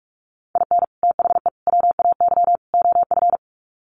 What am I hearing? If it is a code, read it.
Morse code, heard as a keyed tone, about 44 wpm, 700 Hz. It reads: IN THE WAY OF
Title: Sets of 4 Words 45wpm